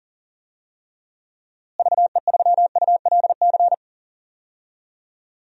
Morse code, heard as a keyed tone, about 39 wpm, 700 Hz. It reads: VE3ULC
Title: DXCC-100 Call Signs - 6 Characters - Repeated 2x - 40wpm